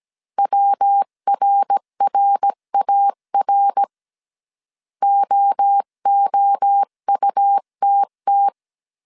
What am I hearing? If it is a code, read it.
Morse code, heard as a keyed tone, about 17 wpm, 800 Hz. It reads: WRRAR OOUTT
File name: CW / DET / 1 3 1500C S A 16wpm 2141